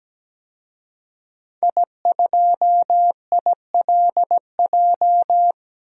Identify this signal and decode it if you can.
Morse code, heard as a keyed tone, about 17 wpm, 700 Hz. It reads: I2ILJ
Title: DXCC-100 Call Signs - 5 Characters - Repeated 2x - 17wpm